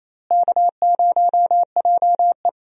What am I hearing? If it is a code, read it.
Morse code, heard as a keyed tone, about 28 wpm, 700 Hz. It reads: K0JE
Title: USA Call Signs - 4 Characters - Repeated 2x - 28wpm